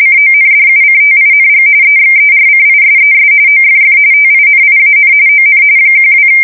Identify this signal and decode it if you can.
RTTY, about 45.45 baud. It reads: CONDUCTING EXERCISES OFF TINIAN ON 12
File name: fsk170 / USS-Massachusetts